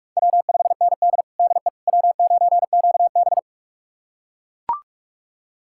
Morse code, heard as a keyed tone, about 42 wpm, 700 Hz. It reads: W5ND DE W9QB
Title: USA Callsign DE Callsign - 4 Characters - 45wpm